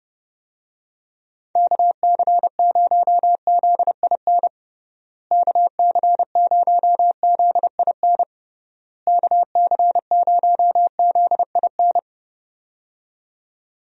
Morse code, heard as a keyed tone, about 30 wpm, 700 Hz. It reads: KC0ZIN KC0ZIN KC0ZIN
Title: USA Call Signs - 6 Characters - Repeated 3x - 30wpm